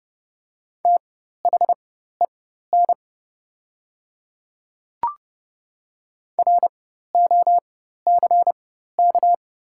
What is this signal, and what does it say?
Morse code, heard as a keyed tone, about 30 wpm, spaced as for 14 wpm, 690 Hz. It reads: THEN ROCK
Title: 4x Character Spacing - Top 500 Words - 30wpm